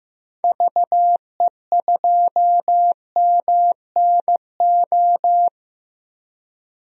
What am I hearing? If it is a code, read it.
Morse code, heard as a keyed tone, about 15 wpm, 700 Hz. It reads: VE2MNO
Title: DXCC-100 Call Signs - 6 Characters - 15wpm